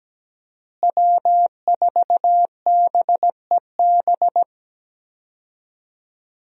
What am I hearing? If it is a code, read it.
Morse code, heard as a keyed tone, about 17 wpm, 700 Hz. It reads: W4BEB
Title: USA Call Signs - 5 Characters - Repeated 2x - 17wpm